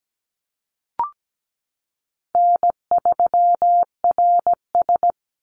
Morse code, heard as a keyed tone, about 17 wpm, 700 Hz. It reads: N3RS
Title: Easy Call Signs Repeated 2x - 17wpm